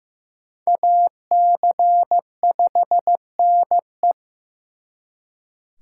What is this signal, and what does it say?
Morse code, heard as a keyed tone, about 19 wpm, 700 Hz. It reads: AC5NE